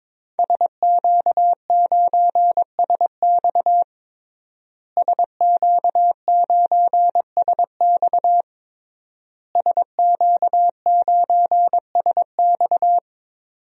Morse code, heard as a keyed tone, about 22 wpm, 700 Hz. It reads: SQ9SX SQ9SX SQ9SX
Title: DXCC-100 Call Signs - 5 Characters - Repeated 3x - 22wpm